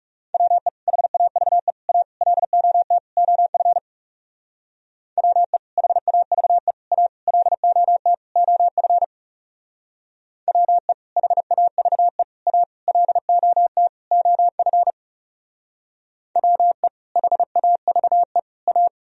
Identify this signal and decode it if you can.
Morse code, heard as a keyed tone, about 36 wpm, 700 Hz. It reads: WE HAVE A LOT OF WE HAVE A LOT OF WE HAVE A LOT OF WE HAVE A
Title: Speed-Racing - Sets of 5 Words 30wpm